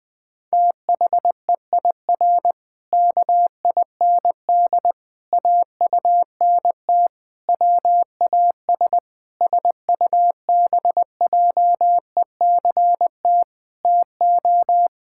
Morse code, heard as a keyed tone, about 21 wpm, 700 Hz. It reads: THEIR KIND AUNT WAS SUBJECT TO
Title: Pangram Challenge v2 - 20wpm